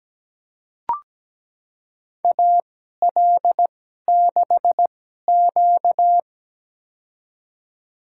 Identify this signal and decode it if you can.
Morse code, heard as a keyed tone, about 17 wpm, spaced as for 12 wpm, 700 Hz. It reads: AL6Q
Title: USA Call Signs - 4 Characters - 2x Character Spacing - 17wpm